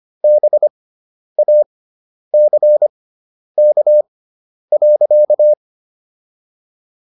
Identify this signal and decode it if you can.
Morse code, heard as a keyed tone, about 25 wpm, spaced as for 10 wpm, 600 Hz. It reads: BACK.